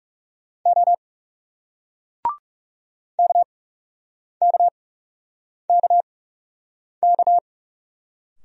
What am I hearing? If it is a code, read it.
Morse code, heard as a keyed tone, about 40 wpm, spaced as for 8 wpm, 700 Hz. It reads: O KKKK